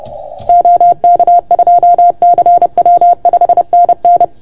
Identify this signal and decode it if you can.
Morse code, heard as a keyed tone, about 30 wpm, 690 Hz. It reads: OK2CW 5NN
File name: Archiv / CUC381 / OK1FGD CUC 381